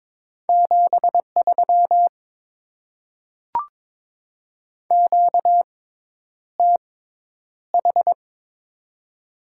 Morse code, heard as a keyed tone, about 22 wpm, spaced as for 10 wpm, 690 Hz. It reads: 73 QTH
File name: Top 20 QSO Elements - 6x Character Spacing - 22wpm